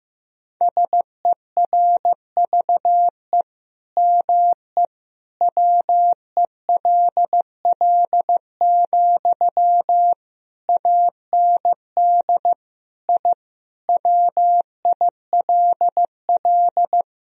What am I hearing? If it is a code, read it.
Morse code, heard as a keyed tone, about 15 wpm, 700 Hz. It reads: SERVE ME WELL, AND I WILL
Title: Sentences from Top 1000 Words - Review Part 2 - 15wpm